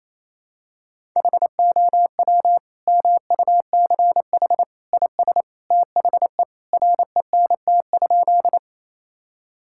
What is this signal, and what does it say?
Morse code, heard as a keyed tone, about 28 wpm, 700 Hz. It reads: HOW MUCH IS THE RENT?
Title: Sentences from Top 1000 Words - Review Part 5 - 28wpm